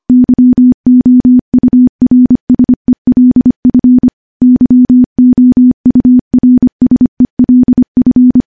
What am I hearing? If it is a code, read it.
Morse code, heard as a keyed tone, about 25 wpm, 250 Hz. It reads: YOURSELF YOURSELF
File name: morse / morse-training-25WPM